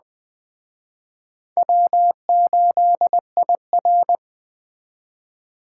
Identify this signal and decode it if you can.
Morse code, heard as a keyed tone, about 20 wpm, 700 Hz. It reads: W8IR
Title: Speed Racing - USA 1x2 Call Signs 17wpm